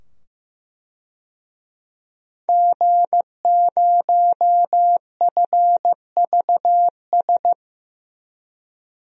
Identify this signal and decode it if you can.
Morse code, heard as a keyed tone, about 15 wpm, 700 Hz. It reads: G0FVS
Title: DXCC-100 Call Signs - 5 Characters - 15wpm